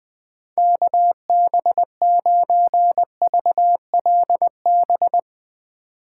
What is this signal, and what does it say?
Morse code, heard as a keyed tone, about 20 wpm, 700 Hz. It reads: KB9VLB
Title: USA Call Signs - 6 Characters - 4x Character Spacing - 20wpm